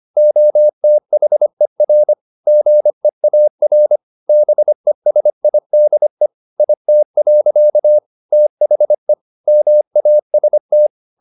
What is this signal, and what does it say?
Morse code, heard as a keyed tone, about 25 wpm, 600 Hz. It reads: OTHER GEAR BESIDE IT. THE MAST
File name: the old man and the sea part 1 25wpm